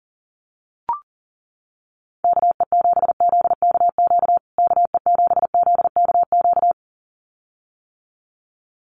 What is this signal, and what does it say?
Morse code, heard as a keyed tone, about 40 wpm, 700 Hz. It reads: KE7ZKQ KE7ZKQ
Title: USA Call Signs Repeated 2x 40wpm